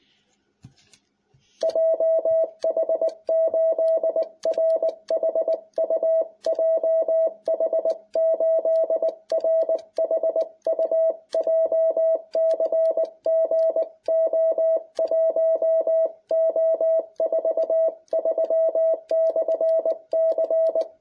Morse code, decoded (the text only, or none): JH8RHU JH8RHU JCG O1O43/C